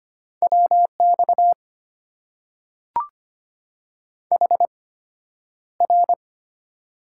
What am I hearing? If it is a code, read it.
Morse code, heard as a keyed tone, about 25 wpm, spaced as for 11 wpm, 700 Hz. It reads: WX HR
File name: Top 90 QSO Elements - 8x Character Spacing - 25wpm